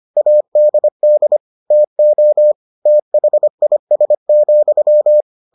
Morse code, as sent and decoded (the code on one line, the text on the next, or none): .- -.. -.. / - --- / - .... .. ... --..--
ADD TO THIS,